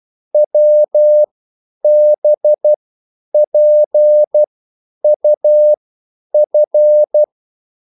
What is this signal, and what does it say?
Morse code, heard as a keyed tone, about 12 wpm, spaced as for 9 wpm, 600 Hz. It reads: WBPUF